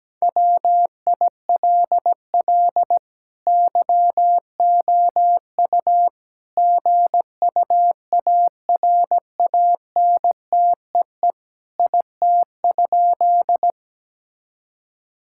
Morse code, heard as a keyed tone, about 17 wpm, 700 Hz. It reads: WILL YOU GUARANTEE IT?
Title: Sentences from Top 1600 Words 17wpm